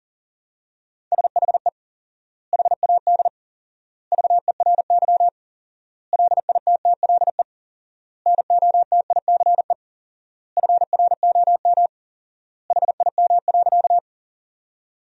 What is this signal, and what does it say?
Morse code, heard as a keyed tone, about 40 wpm, 700 Hz. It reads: SHE HAD VERY LITTLE NOTICE FROM HIM.
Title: 4x Word-Spacing - Sentences from Top 400 Words - 40wpm